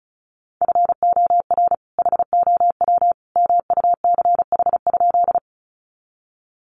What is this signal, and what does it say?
Morse code, heard as a keyed tone, about 34 wpm, 700 Hz. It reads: FOR HOW MUCH?
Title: Sentences from Top 200 Words 35wpm